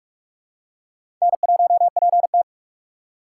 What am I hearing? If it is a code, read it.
Morse code, heard as a keyed tone, about 44 wpm, 700 Hz. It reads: N1PT
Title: USA Call Signs - 2x Character Spacing - 45wpm